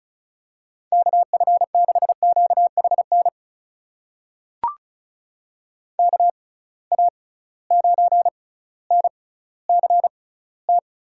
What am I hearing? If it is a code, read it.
Morse code, heard as a keyed tone, about 36 wpm, spaced as for 18 wpm, 700 Hz. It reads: KF6QHN KA9NCT
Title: USA Call Signs - 6 Characters - 6x Character Spacing - 35wpm